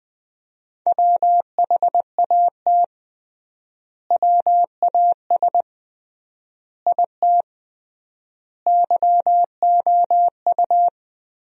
Morse code, heard as a keyed tone, about 20 wpm, 700 Hz. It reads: WHAT WAS IT YOU
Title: 3x Word-Spacing - Sentences from Top 200 Words - 20wpm